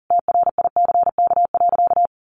Morse code, heard as a keyed tone, about 40 wpm, 700 Hz. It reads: TRICK.
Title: Sentences from Top 1500 Words 40wpm